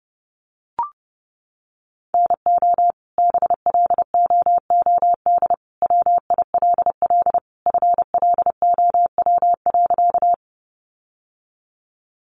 Morse code, heard as a keyed tone, about 30 wpm, 700 Hz. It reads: NO BLOOD WILL FLOW.
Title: Sentences from Top 1200 Words 30wpm